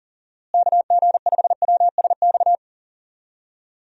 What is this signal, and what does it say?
Morse code, heard as a keyed tone, about 38 wpm, 700 Hz. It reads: KG5WSX
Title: USA Call Signs - 6 Characters - Repeated 3x - 40wpm